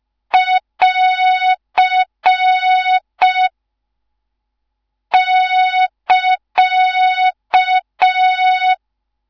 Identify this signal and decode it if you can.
Morse code, heard as a keyed tone, about 5 wpm, 3700 Hz. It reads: <AR> <KA>
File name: Morze-HA2MN / hang / 13IRASJELEK1 HA2MN